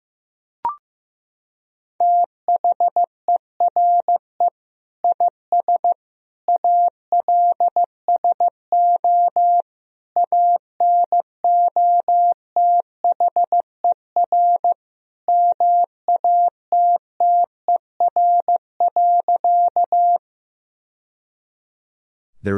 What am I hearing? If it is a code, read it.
Morse code, heard as a keyed tone, about 15 wpm, 700 Hz. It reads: THERE IS ALSO ANOTHER MATTER.
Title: Sentences from Top 1000 Words - Review Part 1 - 15wpm